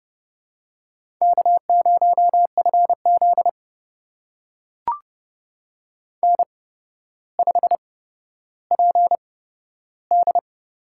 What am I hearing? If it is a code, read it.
Morse code, heard as a keyed tone, about 30 wpm, spaced as for 13 wpm, 700 Hz. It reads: K0FZ N5PD
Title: USA Call Signs - 8x Character Spacing - 30wpm